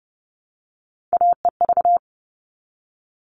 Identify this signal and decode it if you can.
Morse code, heard as a keyed tone, about 30 wpm, 700 Hz. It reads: AEV
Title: All Cut Numbers - 3-Digit 30wpm